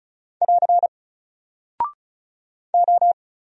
Morse code, heard as a keyed tone, about 36 wpm, 700 Hz. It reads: <AR> O
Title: Single Character (Letter:Number:Punctuation:Prosign) 35wpm